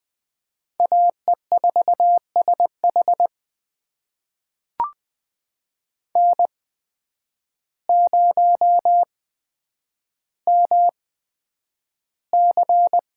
Morse code, heard as a keyed tone, about 20 wpm, spaced as for 9 wpm, 700 Hz. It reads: AE4SH N0MC
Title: USA Call Signs - 5 Characters - 8x Character Spacing - 20wpm